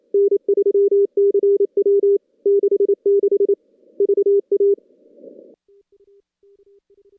Morse code, heard as a keyed tone, about 27 wpm, 400 Hz. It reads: N3CW 66 VA
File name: contests / CW-SS-k5tr- -2100